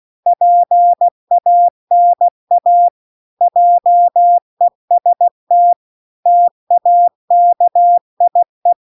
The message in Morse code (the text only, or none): PANA JEST TAKIE